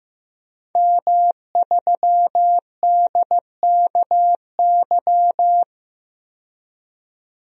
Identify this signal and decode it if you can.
Morse code, heard as a keyed tone, about 15 wpm, 700 Hz. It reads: M3DKY
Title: DXCC-100 Call Signs - 5 Characters - 15wpm